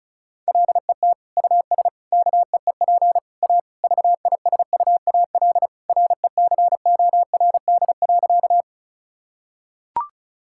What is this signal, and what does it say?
Morse code, heard as a keyed tone, about 34 wpm, 700 Hz. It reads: LET US KEEP A VISUAL RECORD.
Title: Sentences from Top 1000 Words - Review Part 6 - 35wpm